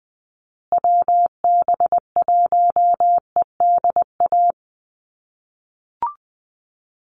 Morse code, heard as a keyed tone, about 20 wpm, 700 Hz. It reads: WB1EDA
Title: USA Call Signs Repeated 2x 20wpm